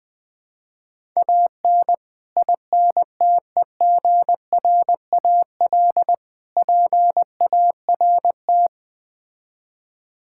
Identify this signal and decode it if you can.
Morse code, heard as a keyed tone, about 20 wpm, 700 Hz. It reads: AN INTEGRAL PART
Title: Build A Phrase - 5-Word Phrases - 20wpm